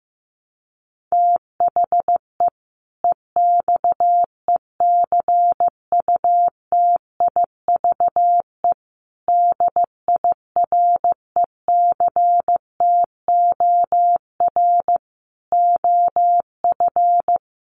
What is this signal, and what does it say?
Morse code, heard as a keyed tone, about 15 wpm, 700 Hz. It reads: THE EXECUTIVE DIRECTOR OF
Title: Sets of 5 Words Encore 15wpm